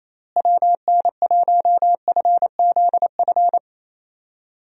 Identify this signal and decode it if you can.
Morse code, heard as a keyed tone, about 28 wpm, 700 Hz. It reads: WN1FZF
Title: USA Call Signs - 6 Characters - Repeated 2x - 28wpm